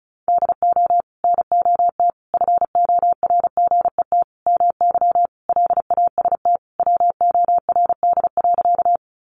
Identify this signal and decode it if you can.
Morse code, heard as a keyed tone, about 34 wpm, 700 Hz. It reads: DO NOT FORGET MY LAST WORD.
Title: Sentences from Top 1200 Words 35wpm